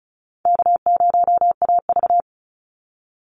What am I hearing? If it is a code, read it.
Morse code, heard as a keyed tone, about 34 wpm, 700 Hz. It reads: K0AV
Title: Easy Call Signs Repeated 2x - 35wpm